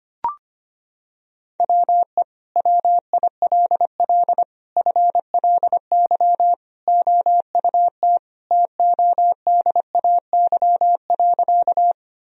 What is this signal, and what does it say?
Morse code, heard as a keyed tone, about 25 wpm, 700 Hz. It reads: WE WILL FLY OUT TODAY.
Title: Sentences from Top 1000 Words - Review Part 2 - 25wpm